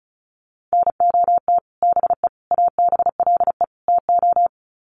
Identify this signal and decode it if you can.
Morse code, heard as a keyed tone, about 34 wpm, 700 Hz. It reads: NOT BE ABLE TO
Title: Sets of 4 Words 35wpm